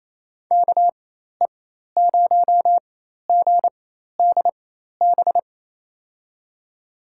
Morse code, heard as a keyed tone, about 28 wpm, spaced as for 13 wpm, 700 Hz. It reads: KE0GDB
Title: USA Call Signs - 6 Characters - 4x Character Spacing - 28wpm